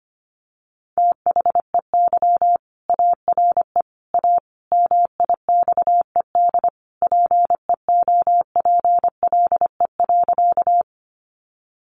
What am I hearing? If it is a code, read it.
Morse code, heard as a keyed tone, about 25 wpm, 700 Hz. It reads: THEY ARE A MIXED PEOPLE.
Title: Sentences from Top 1500 Words 25wpm